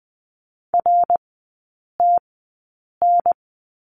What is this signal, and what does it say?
Morse code, heard as a keyed tone, about 20 wpm, spaced as for 8 wpm, 700 Hz. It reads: RTN
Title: 053 Lesson 10e - Letter r - Three Characters 20wpm